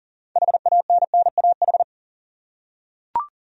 Morse code, heard as a keyed tone, about 40 wpm, 700 Hz. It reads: HANNAH
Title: USA Common Names - 40wpm